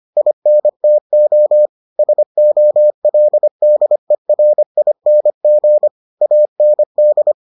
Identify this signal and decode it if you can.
Morse code, heard as a keyed tone, about 25 wpm, 600 Hz. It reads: INTO SOLDERING AND